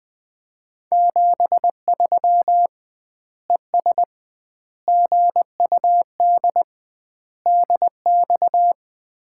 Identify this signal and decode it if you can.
Morse code, heard as a keyed tone, about 20 wpm, 700 Hz. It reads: 73 ES GUD DX
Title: Common QSO Phrases - 2x Word-Spacing - 20wpm